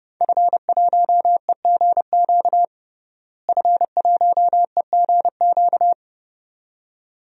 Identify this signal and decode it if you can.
Morse code, heard as a keyed tone, about 30 wpm, 700 Hz. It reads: F1EGQ F1EGQ
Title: DXCC-100 Call Signs - 5 Characters - Repeated 2x - 30wpm